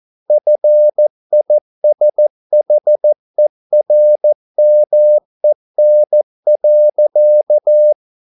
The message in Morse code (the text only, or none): FISHERMEN.